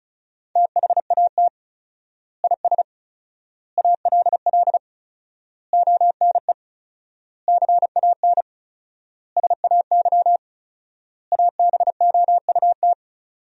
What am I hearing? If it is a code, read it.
Morse code, heard as a keyed tone, about 36 wpm, 700 Hz. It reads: THAT IS ALL ONE CAN SAY ABOUT
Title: 4x Word-Spacing - Sentences from Top 200 Words - 35wpm